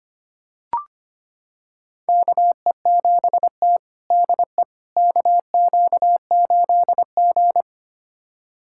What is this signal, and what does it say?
Morse code, heard as a keyed tone, about 25 wpm, 700 Hz. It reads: KE7T DE KQ8G
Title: USA Callsign DE Callsign - 4 Characters - 25wpm